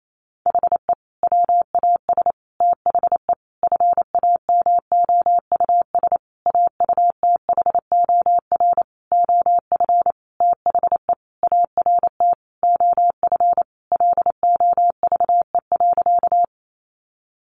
Morse code, heard as a keyed tone, about 28 wpm, 700 Hz. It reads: HE WAS THE FAMOUS AUTHOR OF THE ART OF LOVE.
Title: Sentences from Top 1500 Words 28wpm